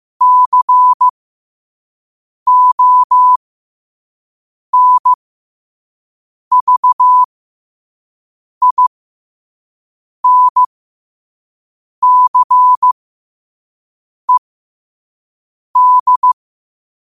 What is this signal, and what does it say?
Morse code, heard as a keyed tone, about 15 wpm, spaced as for 5 wpm, 1000 Hz. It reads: CONVINCED
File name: qotdfiles / 5wpm